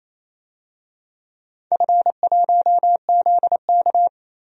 Morse code, heard as a keyed tone, about 28 wpm, 700 Hz. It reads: F1ZK
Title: DXCC-100 Call Signs - 4 Characters - 28wpm